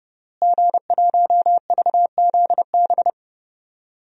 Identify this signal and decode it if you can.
Morse code, heard as a keyed tone, about 32 wpm, 700 Hz. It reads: G1VZB
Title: DXCC-100 Call Signs - 5 Characters - Repeated 3x - 30wpm